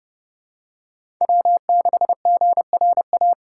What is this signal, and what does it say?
Morse code, heard as a keyed tone, about 30 wpm, 700 Hz. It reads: W6GRA
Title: USA Callsign DE Callsign 30wpm